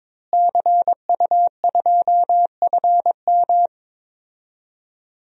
Morse code, heard as a keyed tone, about 22 wpm, 700 Hz. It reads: CU2FM
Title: DXCC-100 Call Signs - 5 Characters - Repeated 3x - 22wpm